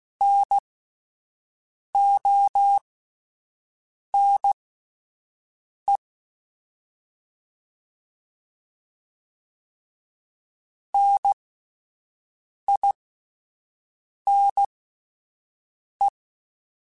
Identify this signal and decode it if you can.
Morse code, heard as a keyed tone, about 16 wpm, spaced as for 5 wpm, 800 Hz. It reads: NONE NINE